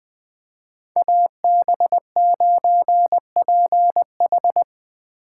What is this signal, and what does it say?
Morse code, heard as a keyed tone, about 20 wpm, 700 Hz. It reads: AB9PH